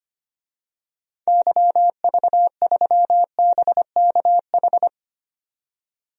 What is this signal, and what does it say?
Morse code, heard as a keyed tone, about 25 wpm, 700 Hz. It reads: YV3BKH
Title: DXCC-100 Call Signs - 6 Characters - 25wpm